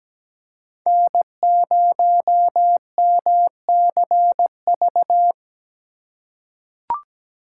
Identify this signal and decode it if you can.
Morse code, heard as a keyed tone, about 17 wpm, 700 Hz. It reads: N0MCV
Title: USA Call Signs - 5 Characters - 4x Character Spacing - 17wpm